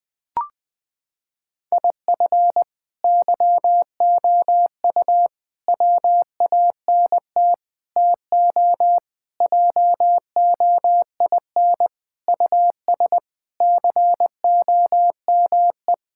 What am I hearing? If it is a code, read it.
Morse code, heard as a keyed tone, about 20 wpm, 700 Hz. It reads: IF YOU WANT TO JOIN US COME
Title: Sentences with 4-Word Phrases - 20wpm